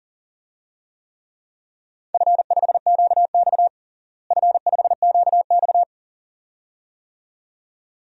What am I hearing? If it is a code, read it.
Morse code, heard as a keyed tone, about 40 wpm, 700 Hz. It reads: F5QX F5QX
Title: DXCC-100 Call Signs - 4 Characters - Repeated 2x - 40wpm